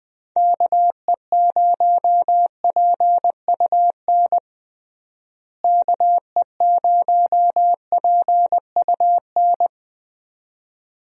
Morse code, heard as a keyed tone, about 20 wpm, 700 Hz. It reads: KE0PUN KE0PUN
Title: USA Call Signs - 6 Characters - Repeated 2x - 20wpm